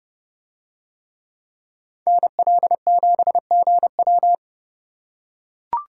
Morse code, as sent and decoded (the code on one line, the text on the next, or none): -. .-.. --... --. .--
NL7GW